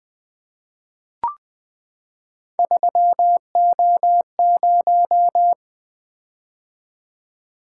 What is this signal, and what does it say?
Morse code, heard as a keyed tone, about 20 wpm, 700 Hz. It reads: 3O0
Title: Three Letter-Number Combinations - 20wpm